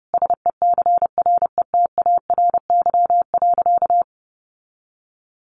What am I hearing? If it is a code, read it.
Morse code, heard as a keyed tone, about 30 wpm, 700 Hz. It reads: SECRETARY.
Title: Sentences from Top 1300 Words 30wpm